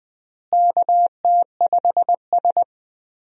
Morse code, heard as a keyed tone, about 20 wpm, 700 Hz. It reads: KT5S
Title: USA Call Signs - 4 Characters - Repeated 3x - 20wpm